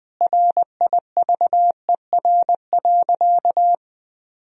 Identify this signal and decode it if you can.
Morse code, heard as a keyed tone, about 20 wpm, 700 Hz. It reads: RIVER.
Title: Sentences from Top 1000 Words - Review Part 1 - 20wpm